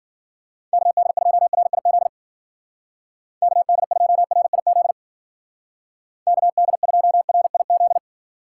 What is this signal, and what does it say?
Morse code, heard as a keyed tone, about 49 wpm, 700 Hz. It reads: KD2RIZ KD2RIZ KD2RIZ
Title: Speed Racing - USA 2x3 Call Signs 40wpm